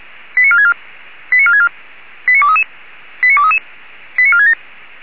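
Five-tone selective call (ZVEI), heard as quick runs of five tones, tones of about 70 ms. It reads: 89464 89464 89233 89233 89467
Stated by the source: ZVEI five-tone selective call 89379 received by SDR.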